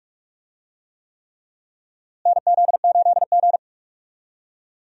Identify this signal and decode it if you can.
Morse code, heard as a keyed tone, about 45 wpm, 700 Hz. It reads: NZ8G
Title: USA Call Signs - 4 Characters - Repeated 3x - 45wpm